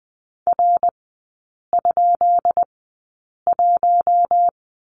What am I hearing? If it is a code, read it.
Morse code, heard as a keyed tone, about 20 wpm, spaced as for 8 wpm, 690 Hz. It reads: R?1
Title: 133 Lesson 22e - Question Mark - Three Characters 20wpm